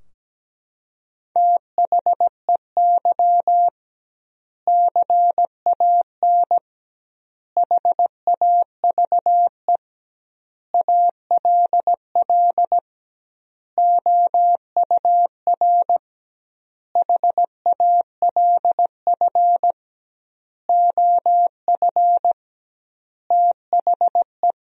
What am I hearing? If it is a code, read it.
Morse code, heard as a keyed tone, about 17 wpm, 700 Hz. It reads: THEY CAN HAVE ALL OUR HALF OF THE
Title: 2x Word-Spacing - Sentences from Top 400 Words - 17wpm